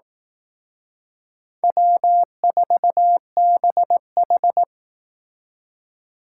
Morse code, heard as a keyed tone, about 18 wpm, 700 Hz. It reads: W4BH
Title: Speed Racing - USA 1x2 Call Signs 15wpm